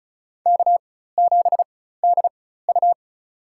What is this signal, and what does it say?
Morse code, heard as a keyed tone, about 36 wpm, spaced as for 16 wpm, 700 Hz. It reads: K7DU
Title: USA Call Signs - 4x Character Spacing - 35wpm